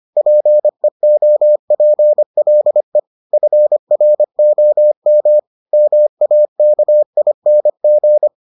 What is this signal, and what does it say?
Morse code, heard as a keyed tone, about 25 wpm, 600 Hz. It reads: PEOPLE FROM MAKING